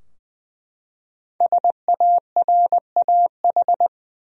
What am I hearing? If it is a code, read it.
Morse code, heard as a keyed tone, about 20 wpm, 700 Hz. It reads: SARAH
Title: USA Common Names - 20wpm